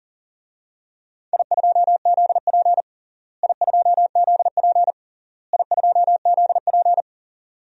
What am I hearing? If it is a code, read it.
Morse code, heard as a keyed tone, about 40 wpm, 700 Hz. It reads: I2ZP I2ZP I2ZP
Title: DXCC-100 Call Signs - 4 Characters - Repeated 3x - 40wpm